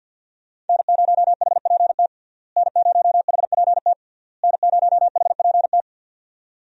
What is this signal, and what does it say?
Morse code, heard as a keyed tone, about 52 wpm, 700 Hz. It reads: N0HPT N0HPT N0HPT
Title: USA Call Signs - 5 Characters - Repeated 3x - 50wpm